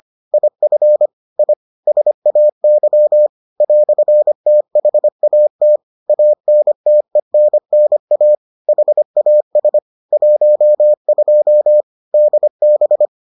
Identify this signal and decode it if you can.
Morse code, heard as a keyed tone, about 25 wpm, 600 Hz. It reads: IF I SAY "THAT ANTENNA HAS 12 DB